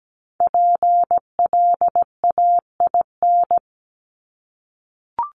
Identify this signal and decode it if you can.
Morse code, heard as a keyed tone, about 17 wpm, 700 Hz. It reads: PLAIN
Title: Top 500 Words Repeated 17wpm